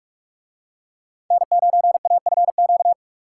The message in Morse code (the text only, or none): N9AFQ